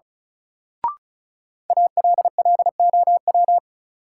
Morse code, heard as a keyed tone, about 34 wpm, 700 Hz. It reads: ALLOW
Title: Sentences from Top 700 Words - New Word Drill - 35wpm